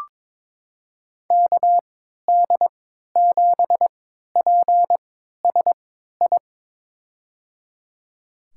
Morse code, heard as a keyed tone, about 22 wpm, spaced as for 13 wpm, 700 Hz. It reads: KD7PSI
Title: USA Call Signs - 6 Characters - 3x Character Spacing - 22wpm